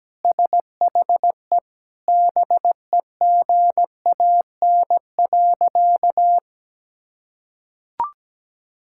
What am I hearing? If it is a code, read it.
Morse code, heard as a keyed tone, about 17 wpm, 700 Hz. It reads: SHE BEGAN.
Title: Sentences from Top 1000 Words - Review Part 5 - 17wpm